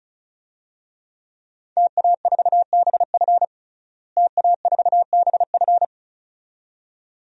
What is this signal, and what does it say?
Morse code, heard as a keyed tone, about 36 wpm, 700 Hz. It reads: TA4BF TA4BF
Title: DXCC-100 Call Signs - 5 Characters - Repeated 2x - 35wpm